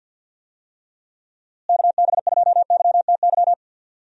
Morse code, heard as a keyed tone, about 47 wpm, 700 Hz. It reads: KB2YTY
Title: USA Call Signs - 6 Characters - Repeated 3x - 50wpm